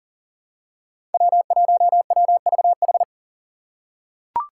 Morse code, heard as a keyed tone, about 40 wpm, 700 Hz. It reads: W1WVH
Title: Five Challenging Letter-Number Combinations 40wpm